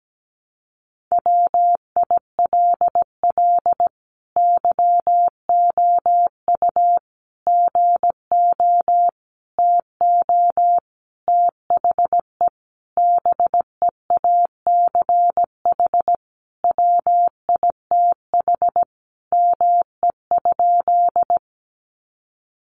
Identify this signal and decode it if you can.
Morse code, heard as a keyed tone, about 17 wpm, 700 Hz. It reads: WILL YOU GO TO THE BEACH WITH ME?
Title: Sentences from Top 1500 Words 17wpm